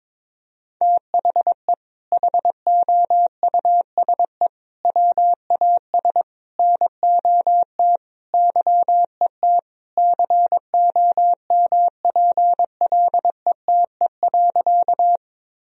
Morse code, heard as a keyed tone, about 22 wpm, 700 Hz. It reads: THE HOUSE WAS NOT YET COMPLETE.
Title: Sentences from Top 1000 Words - Review Part 2 - 22wpm